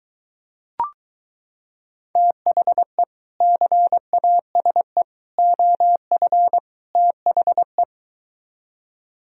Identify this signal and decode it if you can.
Morse code, heard as a keyed tone, about 23 wpm, 700 Hz. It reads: THE CASE OF THE